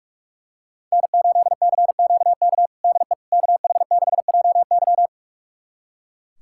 Morse code, heard as a keyed tone, about 44 wpm, 700 Hz. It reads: N8CQK DE KH6JY
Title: USA Callsign DE Callsign - 5 Characters - 45wpm